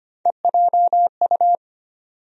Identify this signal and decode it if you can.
Morse code, heard as a keyed tone, about 25 wpm, 700 Hz. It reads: EJU
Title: USA Call Sign Postfixes - 3 Letters 25wpm